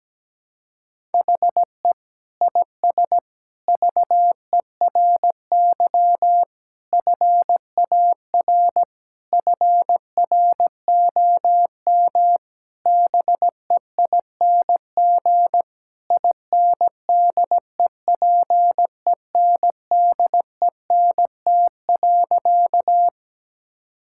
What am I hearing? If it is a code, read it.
Morse code, heard as a keyed tone, about 17 wpm, 700 Hz. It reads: HE IS VERY FAR FROM BEING INDEPENDENT.
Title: Sentences from Top 1000 Words - Review Part 6 - 17wpm